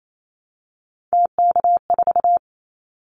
Morse code, heard as a keyed tone, about 28 wpm, 700 Hz. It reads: TK4
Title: Callsign Prefixes for Top 100 DXCC - 28wpm